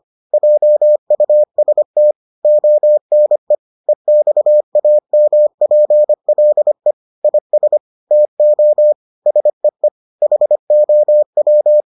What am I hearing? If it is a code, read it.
Morse code, heard as a keyed tone, about 25 wpm, 600 Hz. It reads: JUST ONE EXAMPLE IS TO SEE HOW